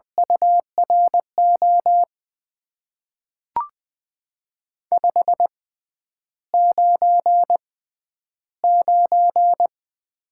Morse code, heard as a keyed tone, about 20 wpm, spaced as for 10 wpm, 700 Hz. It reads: URO 599
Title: Top 90 QSO Elements - 6x Character Spacing - 20wpm